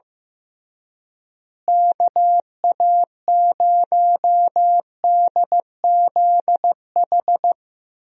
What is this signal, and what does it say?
Morse code, heard as a keyed tone, about 16 wpm, 700 Hz. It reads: KA0DZH
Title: USA Call Signs - 6 Characters - Repeated 3x - 15wpm